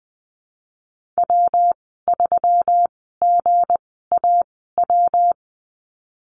Morse code, heard as a keyed tone, about 19 wpm, spaced as for 14 wpm, 700 Hz. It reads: W3GAW
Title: 186 Lesson 30g - Letter b - Call Signs 20wpm